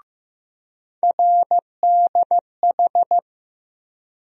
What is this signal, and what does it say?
Morse code, heard as a keyed tone, about 15 wpm, 700 Hz. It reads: RDH
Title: USA Call Sign Postfixes - 3 Letters 15wpm